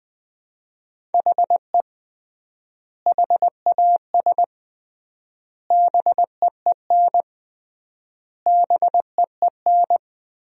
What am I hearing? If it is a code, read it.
Morse code, heard as a keyed tone, about 20 wpm, 700 Hz. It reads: HE HAS BEEN BEEN
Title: 3x Word-Spacing - Sentences from Top 300 Words - 20wpm